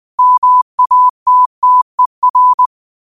Morse code, heard as a keyed tone, about 20 wpm, 1000 Hz. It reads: MATTER
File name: qotdfiles / 20wpm